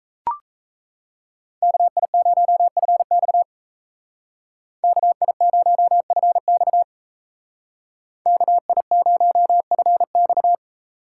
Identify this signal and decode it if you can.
Morse code, heard as a keyed tone, about 36 wpm, 700 Hz. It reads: KI0FX KI0FX KI0FX